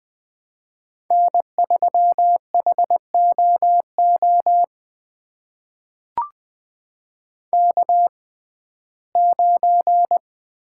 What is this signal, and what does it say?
Morse code, heard as a keyed tone, about 21 wpm, spaced as for 15 wpm, 700 Hz. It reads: N3HOO K9
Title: USA Call Signs - 5 Characters - 6x Character Spacing - 20wpm